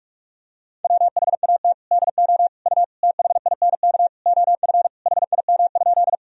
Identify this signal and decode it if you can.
Morse code, heard as a keyed tone, about 46 wpm, 700 Hz. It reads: WHAT DO U THINK OF HIM?
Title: Sentences from Top 200 Words and Abbreviations 45wpm